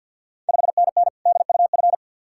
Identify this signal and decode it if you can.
Morse code, heard as a keyed tone, about 53 wpm, 690 Hz. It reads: 5NN DUF
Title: Ontario QSO Party Exchange 50wpm